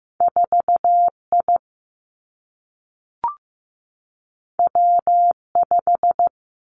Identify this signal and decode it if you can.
Morse code, heard as a keyed tone, about 15 wpm, 700 Hz. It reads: 4 I W 5
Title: Two Challenging Letter-Number Combinations 15wpm